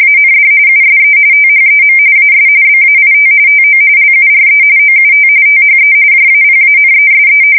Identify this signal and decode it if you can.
RTTY, about 45.45 baud. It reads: AND USE THEM AS BAIT TO LURE OUT THE JAPANESE
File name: fsk170 / battleshipnj